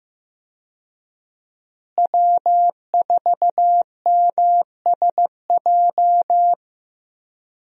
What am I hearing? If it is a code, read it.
Morse code, heard as a keyed tone, about 15 wpm, 700 Hz. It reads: W4MSJ
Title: USA Call Signs - 5 Characters - Repeated 2x - 15wpm